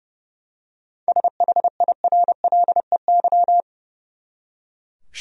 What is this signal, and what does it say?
Morse code, heard as a keyed tone, about 32 wpm, 700 Hz. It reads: SHIRLEY
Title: USA Common Names - 30wpm